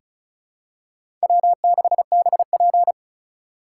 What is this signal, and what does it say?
Morse code, heard as a keyed tone, about 36 wpm, 700 Hz. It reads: W6BP
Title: USA Call Signs - 4 Characters - 2x Character Spacing - 35wpm